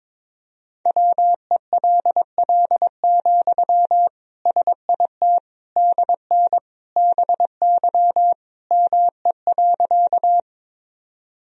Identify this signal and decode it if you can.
Morse code, heard as a keyed tone, about 22 wpm, 700 Hz. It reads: WELL, SIT DN BY ME.
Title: Sentences from Top 500 Words and Abbreviations 22wpm